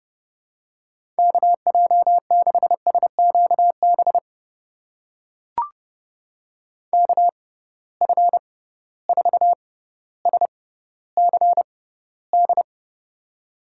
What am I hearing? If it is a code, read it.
Morse code, heard as a keyed tone, about 30 wpm, spaced as for 15 wpm, 700 Hz. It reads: KJ6SQB KF4SCD
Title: USA Call Signs - 6 Characters - 6x Character Spacing - 30wpm